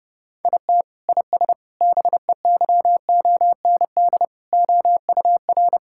Morse code, heard as a keyed tone, about 30 wpm, 700 Hz. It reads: IT IS BEYOND OUR
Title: Sentences from Top 1000 Words - Review Part 6 - 30wpm